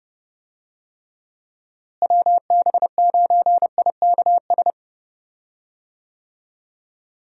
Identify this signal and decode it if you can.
Morse code, heard as a keyed tone, about 29 wpm, 700 Hz. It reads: WB9IKS
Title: USA Call Signs - 6 Characters - Repeated 2x - 30wpm